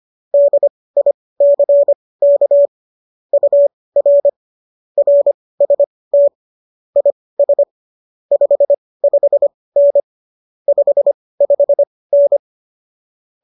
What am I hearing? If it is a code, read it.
Morse code, heard as a keyed tone, about 25 wpm, spaced as for 18 wpm, 570 Hz. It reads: DICK UR RST IS 55N 55N